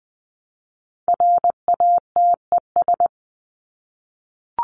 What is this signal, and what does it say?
Morse code, heard as a keyed tone, about 20 wpm, 700 Hz. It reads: RATES
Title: ES:S Suffix - Top 100 Words with Base Words 20wpm